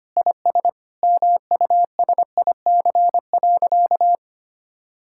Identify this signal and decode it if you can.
Morse code, heard as a keyed tone, about 25 wpm, 700 Hz. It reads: IS MUSIC.
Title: Sentences from Top 1600 Words 25wpm